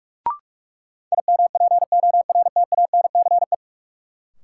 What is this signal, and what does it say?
Morse code, heard as a keyed tone, about 42 wpm, 700 Hz. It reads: IMPORTANCE
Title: Sets of 3 Words - New Word Drill - 45wpm